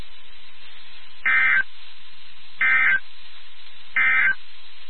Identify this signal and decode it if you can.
Emergency Alert System (SAME) tones, heard as harsh buzzing bursts. It is heard from 1.1 to 4.4 s.
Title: NOAA Weather Radio WXK79